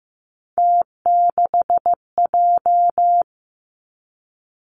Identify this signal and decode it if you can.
Morse code, heard as a keyed tone, about 15 wpm, 700 Hz. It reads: T6J
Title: Three Challenging Letter-Number Combinations 15wpm